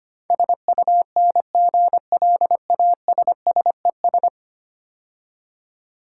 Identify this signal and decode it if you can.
Morse code, heard as a keyed tone, about 25 wpm, 700 Hz. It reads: SUNGLASSES
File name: Easy Words - Speed-Racing with Hints - 25wpm